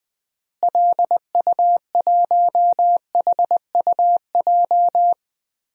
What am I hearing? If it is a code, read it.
Morse code, heard as a keyed tone, about 20 wpm, 700 Hz. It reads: LU1HUJ
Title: DXCC-100 Call Signs - 6 Characters - Repeated 3x - 20wpm